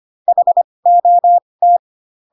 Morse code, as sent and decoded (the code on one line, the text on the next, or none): .... --- -
HOT